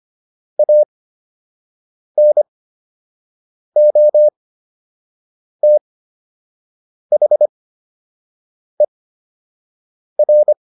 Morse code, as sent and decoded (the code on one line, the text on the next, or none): .- -. --- - .... .-.
ANOTHER